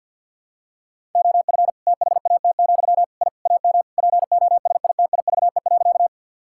Morse code, heard as a keyed tone, about 50 wpm, 700 Hz. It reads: OF THAT, I AM POSITIVE.
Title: Sentences from Top 1000 Words - Review Part 5 - 50wpm